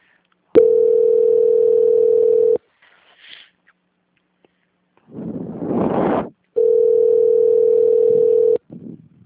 Ringback tone (North America), 440 + 480 Hz.